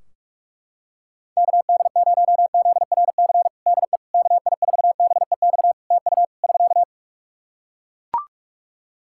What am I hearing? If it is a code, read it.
Morse code, heard as a keyed tone, about 42 wpm, 700 Hz. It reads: KD0ZRC DE KI4BEX TU <SK>
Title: Practice QSO - Exchange 4 - Ending - 45wpm